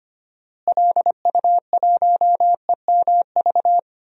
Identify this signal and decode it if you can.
Morse code, heard as a keyed tone, about 25 wpm, 700 Hz. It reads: LU1EMV
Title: DXCC-100 Call Signs - 6 Characters - Repeated 2x - 25wpm